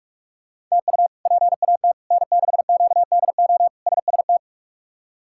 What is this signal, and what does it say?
Morse code, heard as a keyed tone, about 46 wpm, 700 Hz. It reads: TU PAT N6QDO SST
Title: K1USN SST QSO Practice - 45wpm